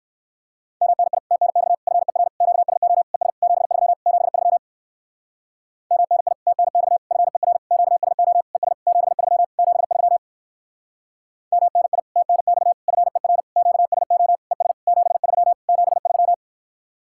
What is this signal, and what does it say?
Morse code, heard as a keyed tone, about 65 wpm, 700 Hz. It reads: MNI TNX FER QSO ES 73 73 MNI TNX FER QSO ES 73 73 MNI TNX FER QSO ES 73 73